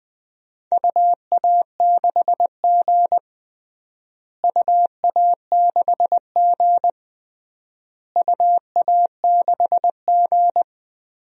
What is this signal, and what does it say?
Morse code, heard as a keyed tone, about 21 wpm, 700 Hz. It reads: UA6G UA6G UA6G
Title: DXCC-100 Call Signs - 4 Characters - Repeated 3x - 20wpm